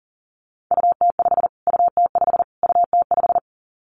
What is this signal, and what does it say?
Morse code, heard as a keyed tone, about 40 wpm, 700 Hz. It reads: UT5 UT5 UT5
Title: Callsign Prefixes for Top 25 DXCC - Repeated 3x - 40wpm